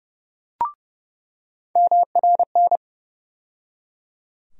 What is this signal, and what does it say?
Morse code, heard as a keyed tone, about 30 wpm, 700 Hz. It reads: MRN
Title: USA Call Sign Postfixes - 3 Letters 30wpm